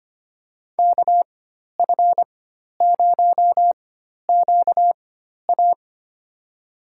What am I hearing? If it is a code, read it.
Morse code, heard as a keyed tone, about 25 wpm, spaced as for 12 wpm, 700 Hz. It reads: KF0QA
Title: USA Call Signs - 5 Characters - 4x Character Spacing - 25wpm